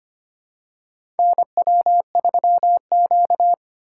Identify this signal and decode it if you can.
Morse code, heard as a keyed tone, about 25 wpm, 700 Hz. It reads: NW3Q
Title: USA Call Signs - 4 Characters - 2x Character Spacing - 25wpm